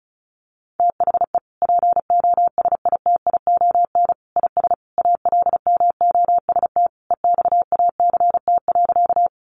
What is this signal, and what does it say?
Morse code, heard as a keyed tone, about 34 wpm, 700 Hz. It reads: THE POSITION IS ALMOST EXACT.
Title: Sentences from Top 1300 Words 35wpm